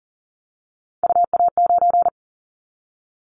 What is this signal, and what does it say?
Morse code, heard as a keyed tone, about 43 wpm, 700 Hz. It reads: UA9
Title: Callsign Prefixes for Top 50 DXCC - 40wpm